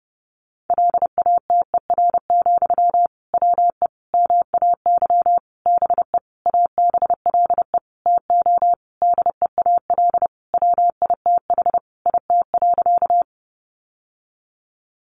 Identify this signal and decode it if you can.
Morse code, heard as a keyed tone, about 30 wpm, 700 Hz. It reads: LATER, WE MAY BE ABLE TO DEAL WITH IT.
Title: Sentences from Top 800 Words 30wpm